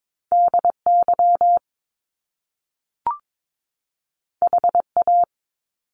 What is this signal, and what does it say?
Morse code, heard as a keyed tone, about 22 wpm, 700 Hz. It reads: DY HA
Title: Top Two Letter Combinations in Words 22wpm